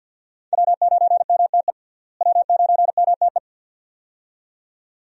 Morse code, heard as a keyed tone, about 48 wpm, 700 Hz. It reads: W9MTE W9MTE
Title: USA Call Signs - 5 Characters - Repeated 2x - 50wpm